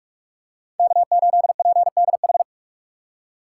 Morse code, heard as a keyed tone, about 46 wpm, 700 Hz. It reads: K8PDH
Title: USA Call Signs - 5 Characters - 2x Character Spacing - 45wpm